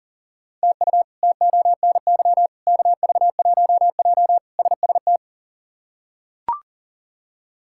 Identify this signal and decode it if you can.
Morse code, heard as a keyed tone, about 40 wpm, 700 Hz. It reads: TU TONY KV1J SST
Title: K1USN SST QSO Practice - 40wpm